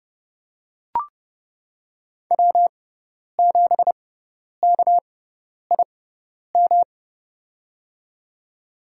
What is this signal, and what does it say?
Morse code, heard as a keyed tone, about 30 wpm, spaced as for 10 wpm, 700 Hz. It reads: W7KIM
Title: USA Call Signs - 5 Characters - 6x Character Spacing - 30wpm